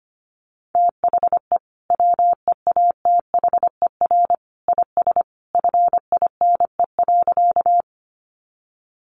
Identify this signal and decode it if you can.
Morse code, heard as a keyed tone, about 25 wpm, 700 Hz. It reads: THE WEATHER IS FINE.
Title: Sentences from Top 800 Words 25wpm